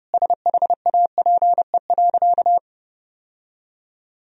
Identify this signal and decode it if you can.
Morse code, heard as a keyed tone, about 30 wpm, 700 Hz. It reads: SHAPE.
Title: Sentences from Top 1000 Words - Review Part 3 - 30wpm